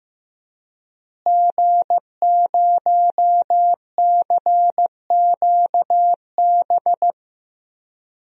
Morse code, heard as a keyed tone, about 15 wpm, 700 Hz. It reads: G0CQB
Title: DXCC-100 Call Signs - 5 Characters - Repeated 2x - 15wpm